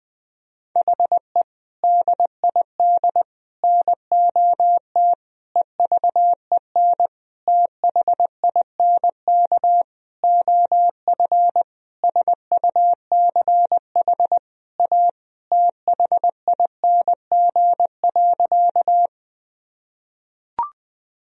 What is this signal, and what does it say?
Morse code, heard as a keyed tone, about 21 wpm, 700 Hz. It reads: HE DID NOT EVEN THINK OF SUCH A THING.
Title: Sentences from Top 1000 Words - Review Part 1 - 20wpm